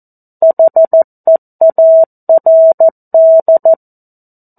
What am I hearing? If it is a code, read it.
Morse code, heard as a keyed tone, about 14 wpm, 650 Hz. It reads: HEARD
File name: 02-Chapter-02 14 WPM